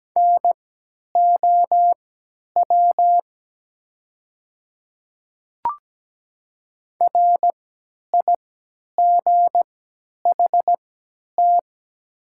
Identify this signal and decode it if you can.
Morse code, heard as a keyed tone, about 17 wpm, spaced as for 10 wpm, 700 Hz. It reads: NOW RIGHT
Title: 3x Character Spacing - Top 200 Words - 17wpm